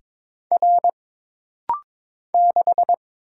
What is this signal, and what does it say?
Morse code, heard as a keyed tone, about 22 wpm, 700 Hz. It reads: R 6
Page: This rapid-fire timing is geared to push your ICR (Instant Character Recognition) skills.